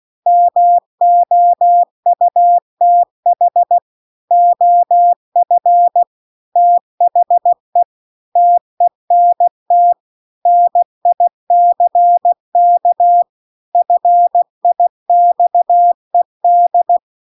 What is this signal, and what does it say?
Morse code, heard as a keyed tone, about 16 wpm, 700 Hz. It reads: MOUTH OF THE TENT NICK FIXED